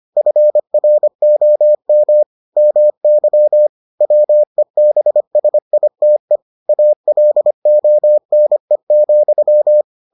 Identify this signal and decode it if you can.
Morse code, heard as a keyed tone, about 25 wpm, 600 Hz. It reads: FROM MY WEBSITE ALONE,